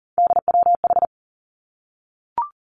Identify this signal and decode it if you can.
Morse code, heard as a keyed tone, about 38 wpm, 700 Hz. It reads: DWH
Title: Three Challenging Letter-Number Combinations 40wpm